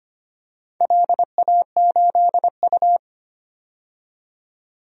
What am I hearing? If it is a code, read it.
Morse code, heard as a keyed tone, about 25 wpm, 700 Hz. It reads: LA8U
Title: DXCC-100 Call Signs - 4 Characters - 25wpm